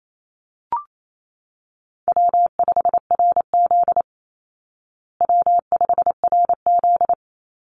Morse code, heard as a keyed tone, about 28 wpm, 700 Hz. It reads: W5RZ W5RZ
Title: Easy Call Signs Repeated 3x - 28wpm